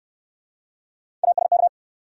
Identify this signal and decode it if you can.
Morse code, heard as a keyed tone, about 68 wpm, 700 Hz. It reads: ASK